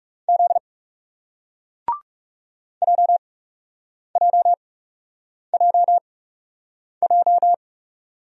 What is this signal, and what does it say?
Morse code, heard as a keyed tone, about 41 wpm, spaced as for 8 wpm, 700 Hz. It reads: Z JJJJ